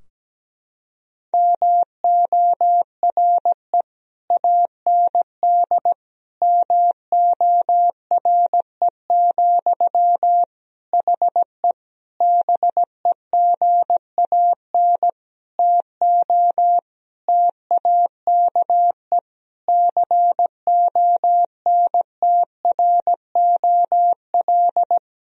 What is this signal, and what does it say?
Morse code, heard as a keyed tone, about 17 wpm, 700 Hz. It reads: MORE AND MORE, HE BEGAN TO TAKE CONTROL
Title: Sentences from Top 1000 Words - Review Part 3 - 17wpm